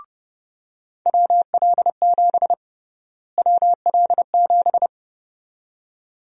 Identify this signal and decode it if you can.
Morse code, heard as a keyed tone, about 30 wpm, 700 Hz. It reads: WL7 WL7
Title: USA Call Sign Prefixes - Repeated 2x - 30wpm